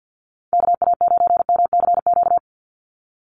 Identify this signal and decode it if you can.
Morse code, heard as a keyed tone, about 48 wpm, 700 Hz. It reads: KA9MCQ
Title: USA Call Signs 50wpm